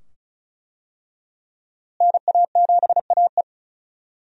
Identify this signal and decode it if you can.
Morse code, heard as a keyed tone, about 36 wpm, 700 Hz. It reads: NA7AE
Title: USA Call Signs - 5 Characters - Repeated 2x - 35wpm